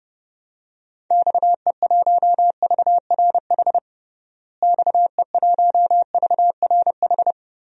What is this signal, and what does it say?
Morse code, heard as a keyed tone, about 30 wpm, 700 Hz. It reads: XE1VRH XE1VRH
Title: DXCC-100 Call Signs - 6 Characters - Repeated 3x - 30wpm